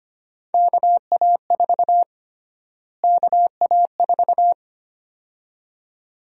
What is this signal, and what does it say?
Morse code, heard as a keyed tone, about 25 wpm, 700 Hz. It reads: KA4 KA4
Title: USA Call Sign Prefixes - Repeated 2x - 25wpm